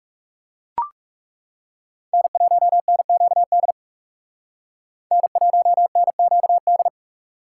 Morse code, heard as a keyed tone, about 38 wpm, 700 Hz. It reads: N1NQD N1NQD